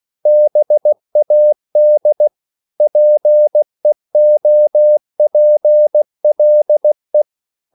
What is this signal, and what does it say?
Morse code, heard as a keyed tone, about 16 wpm, 600 Hz. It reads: BAD PEOPLE
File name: CW / Practice / EnglishQuotations0116